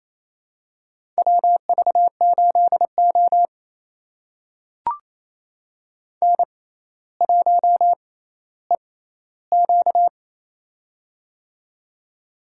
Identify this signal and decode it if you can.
Morse code, heard as a keyed tone, about 28 wpm, spaced as for 14 wpm, 700 Hz. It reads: WV8O N1EQ
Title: USA Call Signs - 4 Characters - 6x Character Spacing - 28wpm